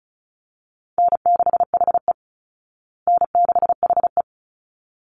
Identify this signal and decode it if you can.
Morse code, heard as a keyed tone, about 34 wpm, 700 Hz. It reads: N6HE N6HE
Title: Easy Call Signs Repeated 2x - 35wpm